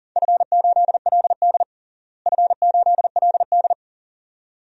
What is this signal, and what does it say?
Morse code, heard as a keyed tone, about 42 wpm, 690 Hz. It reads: F8LD F8LD
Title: DXCC-100 Call Signs - 4 Characters - Repeated 2x - 40wpm